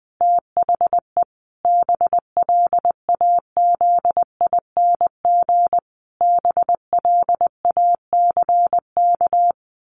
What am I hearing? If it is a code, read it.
Morse code, heard as a keyed tone, about 20 wpm, 700 Hz. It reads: THE BLAZING BLACK
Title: A Sound Of Thunder 20wpm